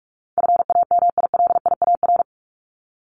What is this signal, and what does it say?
Morse code, heard as a keyed tone, about 44 wpm, 700 Hz. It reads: FAMILIAR
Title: Top 751-1000 Words 45wpm